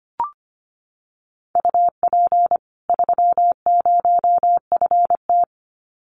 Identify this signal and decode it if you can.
Morse code, heard as a keyed tone, about 25 wpm, 700 Hz. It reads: UP 30FT
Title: Common QSO Phrases 25wpm